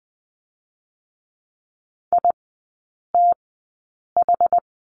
Morse code, heard as a keyed tone, about 20 wpm, spaced as for 8 wpm, 690 Hz. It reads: ITH
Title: 060 Lesson 11e - Letter h - Three Characters 20wpm